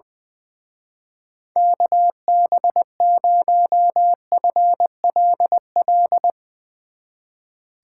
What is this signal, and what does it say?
Morse code, heard as a keyed tone, about 20 wpm, 700 Hz. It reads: KB0FLL
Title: Speed Racing - USA 2x3 Call Signs 20wpm